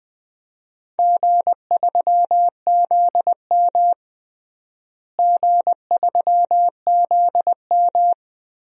Morse code, heard as a keyed tone, about 20 wpm, 700 Hz. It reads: G3ZM G3ZM
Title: DXCC-100 Call Signs - 4 Characters - Repeated 3x - 20wpm